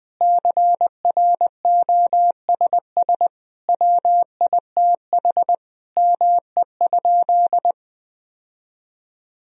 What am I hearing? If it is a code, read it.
Morse code, heard as a keyed tone, about 21 wpm, 700 Hz. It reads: CROSS WITH ME?
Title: Sentences from Top 1000 Words - Review Part 1 - 20wpm